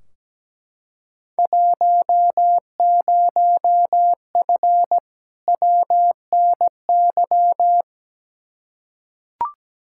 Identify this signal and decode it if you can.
Morse code, heard as a keyed tone, about 17 wpm, 700 Hz. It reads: 10F WNY
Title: Field Day Exchanges Encore - 17wpm